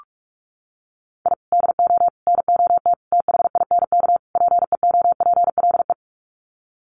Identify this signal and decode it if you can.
Morse code, heard as a keyed tone, about 44 wpm, 700 Hz. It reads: I DO NOT THINK PEOPLE
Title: Sets of 5 Words Encore2 45wpm